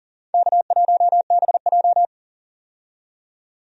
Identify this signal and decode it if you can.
Morse code, heard as a keyed tone, about 40 wpm, 700 Hz. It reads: K1BJ
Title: USA Call Signs - 4 Characters - 40wpm